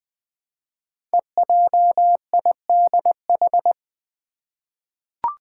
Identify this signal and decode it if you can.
Morse code, heard as a keyed tone, about 20 wpm, 700 Hz. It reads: EJIDH
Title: Code Groups - 5 Letters - 20wpm